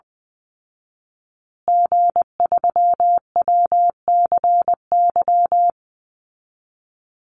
Morse code, heard as a keyed tone, about 20 wpm, 700 Hz. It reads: G3WCY
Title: Call Signs Repeated 3x - 20wpm